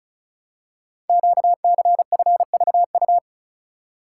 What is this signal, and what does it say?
Morse code, heard as a keyed tone, about 36 wpm, 690 Hz. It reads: QCFVU
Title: Code Groups - 5 Letters - 35wpm